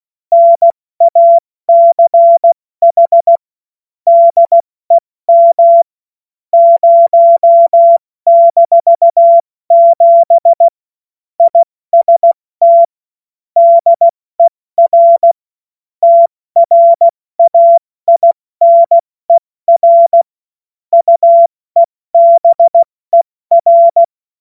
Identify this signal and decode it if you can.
Morse code, heard as a keyed tone, about 16 wpm, 680 Hz. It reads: NACH DEM 0-7 IST DER TRAINER UEBER
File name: Morsekurs / Witze / Witz 27. April 2021, 70 F80